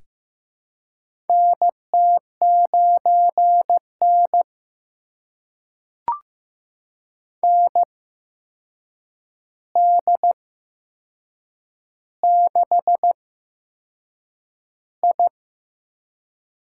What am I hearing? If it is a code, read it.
Morse code, heard as a keyed tone, about 15 wpm, spaced as for 6 wpm, 700 Hz. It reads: NT9N ND6I
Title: USA Call Signs - 4 Characters - 8x Character Spacing - 15wpm